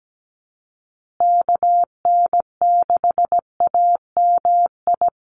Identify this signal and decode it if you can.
Morse code, heard as a keyed tone, about 17 wpm, 700 Hz. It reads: KN6AMI
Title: USA Call Signs Repeated 3x 17wpm